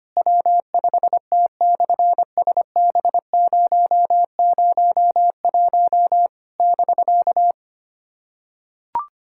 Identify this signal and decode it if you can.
Morse code, heard as a keyed tone, about 24 wpm, 700 Hz. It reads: W5T/SB001 <BK>
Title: SOTA QSO Practice 25wpm